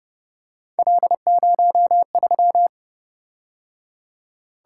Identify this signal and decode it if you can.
Morse code, heard as a keyed tone, about 29 wpm, 700 Hz. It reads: L03
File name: Speed-Racing - Three Letter-Number Combinations - 25wpm